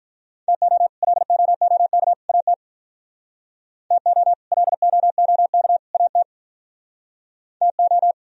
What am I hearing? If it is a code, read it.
Morse code, heard as a keyed tone, about 44 wpm, 700 Hz. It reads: TO LOOK AT TO LOOK AT TO